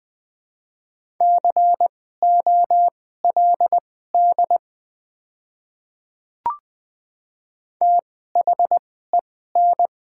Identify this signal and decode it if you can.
Morse code, heard as a keyed tone, about 20 wpm, spaced as for 14 wpm, 700 Hz. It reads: COLD THEN
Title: 2x Character Spacing - Top 500 Words - 20wpm